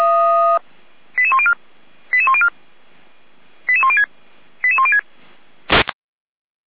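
Five-tone selective call (ZVEI), heard as quick runs of five tones, tones of about 70 ms. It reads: 88104 88104 88106 88106
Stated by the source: ZVEI five-tone selective call received on an SDR.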